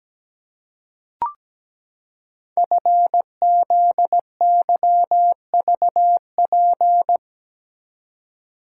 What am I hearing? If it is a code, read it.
Morse code, heard as a keyed tone, about 17 wpm, 690 Hz. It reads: FZYVP
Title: Code Groups - 5 Letter:Numbers - 17wpm